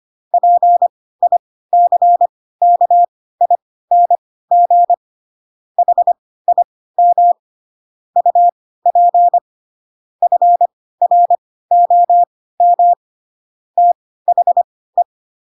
Morse code, heard as a keyed tone, about 25 wpm, spaced as for 16 wpm, 700 Hz. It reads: PICKING HIM UP FROM THE
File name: Pinocchio / Pinocchio ch7 10 25 ik1hge